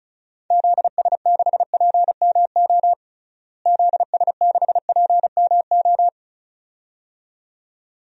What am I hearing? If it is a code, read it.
Morse code, heard as a keyed tone, about 34 wpm, 700 Hz. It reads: ZS6PMO ZS6PMO
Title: DXCC-100 Call Signs - 6 Characters - Repeated 2x - 35wpm